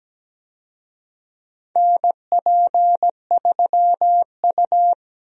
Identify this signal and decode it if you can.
Morse code, heard as a keyed tone, about 17 wpm, 700 Hz. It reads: NP3U
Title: USA Call Signs - 4 Characters - Repeated 2x - 17wpm